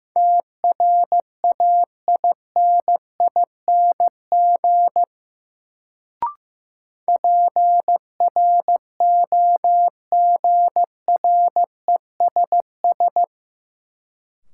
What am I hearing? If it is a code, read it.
Morse code, heard as a keyed tone, about 15 wpm, 700 Hz. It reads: TRAINING PROGRESS
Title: Sentences from Top 1400 Words - New Word Drill Alt - 15wpm